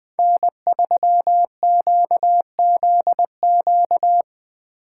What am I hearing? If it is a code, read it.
Morse code, heard as a keyed tone, about 20 wpm, 700 Hz. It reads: N3QZQ
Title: USA Call Signs - 5 Characters - 8x Character Spacing - 20wpm